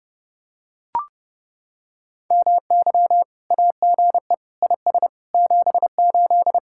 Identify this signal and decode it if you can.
Morse code, heard as a keyed tone, about 30 wpm, 700 Hz. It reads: MY AGE IS 78